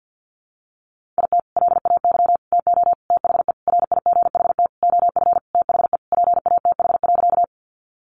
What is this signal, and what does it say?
Morse code, heard as a keyed tone, about 50 wpm, 700 Hz. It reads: IT LAY TO THE RIGHT OF THE PATH.
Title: Sentences from Top 1400 Words 50wpm